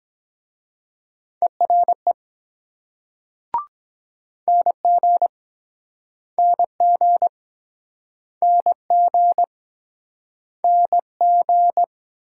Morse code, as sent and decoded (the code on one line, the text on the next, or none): . .-. / -. --. / -. --. / -. --. / -. --.
ERE NG NG NG NG